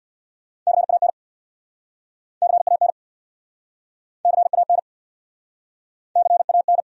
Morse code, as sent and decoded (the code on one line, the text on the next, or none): -.-. .- -. / -.-. .- -. / -.-. .- -. / -.-. .- -.
CAN CAN CAN CAN